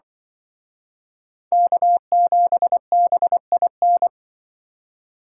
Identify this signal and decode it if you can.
Morse code, heard as a keyed tone, about 24 wpm, 700 Hz. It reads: K7BIN
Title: Speed Racing - USA 1x3 Call Signs 20wpm